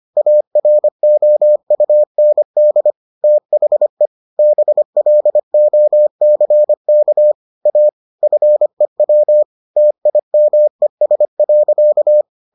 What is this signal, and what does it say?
Morse code, heard as a keyed tone, about 25 wpm, 600 Hz. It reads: AROUND THE BLOCK A FEW TIMES.